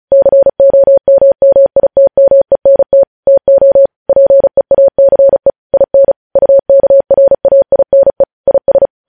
Morse code, heard as a keyed tone, about 36 wpm, 550 Hz. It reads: COMMITMENT TO PEACE IN UKRAINE IS
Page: News Headlines in Morse Code at 35 WPM